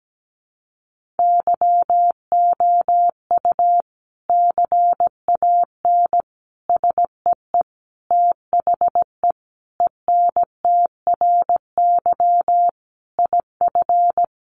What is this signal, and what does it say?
Morse code, heard as a keyed tone, about 17 wpm, 700 Hz. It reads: YOU CAN SEE THE ENTRY IF
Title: Sentences from Top 1500 Words 17wpm